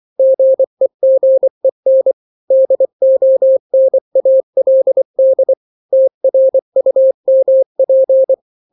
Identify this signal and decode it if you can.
Morse code, heard as a keyed tone, about 24 wpm, 530 Hz. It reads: GEGEN DONALD TRUMP
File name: daily / 23wpm